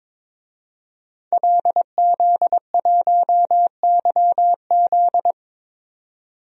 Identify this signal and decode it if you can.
Morse code, heard as a keyed tone, about 22 wpm, 700 Hz. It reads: LZ1YZ
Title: DXCC-100 Call Signs - 5 Characters - Repeated 2x - 22wpm